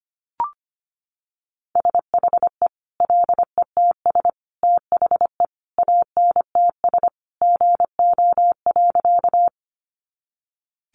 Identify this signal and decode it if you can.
Morse code, heard as a keyed tone, about 25 wpm, 700 Hz. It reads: SHE LETS THE ANTS GO.
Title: Additional Easy Phrases 25wpm